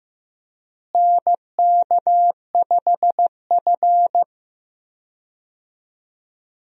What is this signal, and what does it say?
Morse code, heard as a keyed tone, about 15 wpm, 700 Hz. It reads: NK5F
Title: USA Call Signs - 4 Characters - Repeated 3x - 15wpm